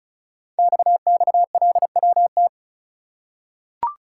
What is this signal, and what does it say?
Morse code, heard as a keyed tone, about 36 wpm, 700 Hz. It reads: XXLWT
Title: Code Groups - 5 Letters - 35wpm